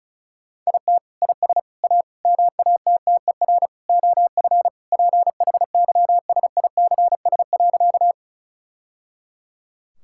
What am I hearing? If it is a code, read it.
Morse code, heard as a keyed tone, about 36 wpm, 700 Hz. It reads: IT IS A MATTER OF PHYSICS.
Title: Sentences from Top 1000 Words - Review Part 5 - 35wpm